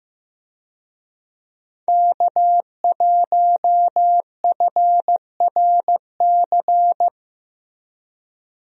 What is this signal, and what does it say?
Morse code, heard as a keyed tone, about 15 wpm, 700 Hz. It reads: K1FRC
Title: USA Call Signs - 5 Characters - Repeated 2x - 15wpm